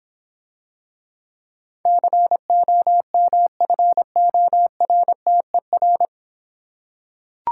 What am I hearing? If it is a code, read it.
Morse code, heard as a keyed tone, about 26 wpm, 700 Hz. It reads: COMFORTER